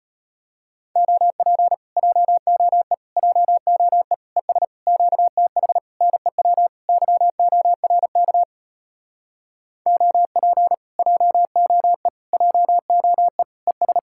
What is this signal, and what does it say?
Morse code, heard as a keyed tone, about 33 wpm, 700 Hz. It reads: OP JOE JOE ES QTH NEW YORK OP JOE JOE ES